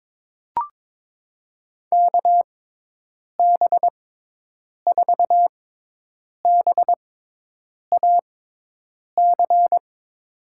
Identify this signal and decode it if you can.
Morse code, heard as a keyed tone, about 22 wpm, spaced as for 8 wpm, 700 Hz. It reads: KB4BAC
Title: USA Call Signs - 6 Characters - 6x Character Spacing - 22wpm